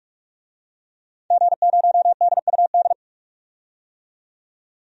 Morse code, heard as a keyed tone, about 46 wpm, 700 Hz. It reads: G0DUD
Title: DXCC-100 Call Signs - 5 Characters - 45wpm